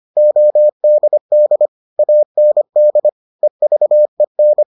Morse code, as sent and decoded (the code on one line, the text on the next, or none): --- -.. -.. / .- -. -.. / . ...- . -.
ODD AND EVEN